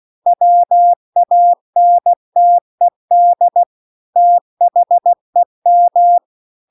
Morse code, heard as a keyed tone, about 16 wpm, 700 Hz. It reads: WANTED THEM